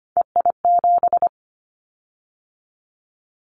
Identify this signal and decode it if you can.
Morse code, heard as a keyed tone, about 25 wpm, 700 Hz. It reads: EI7
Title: Callsign Prefixes for Top 50 DXCC - 25wpm